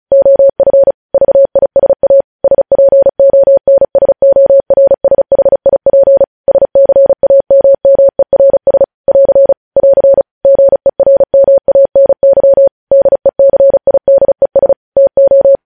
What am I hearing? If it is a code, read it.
Morse code, heard as a keyed tone, about 36 wpm, 550 Hz. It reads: OF VISA SPONSORSHIP SCAMMERS <AR> <AR> GERMANY DECIDES TO
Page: News headlines in Morse code at 35 wpm